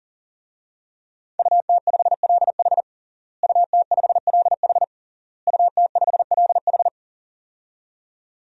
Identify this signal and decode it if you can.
Morse code, heard as a keyed tone, about 40 wpm, 700 Hz. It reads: UT5LH UT5LH UT5LH
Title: DXCC-100 Call Signs - 5 Characters - Repeated 3x - 40wpm